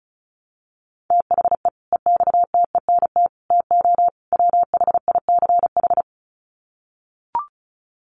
Morse code, heard as a keyed tone, about 34 wpm, 700 Hz. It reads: THE EXTENT TO WHICH
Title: Sets of 4 Words 35wpm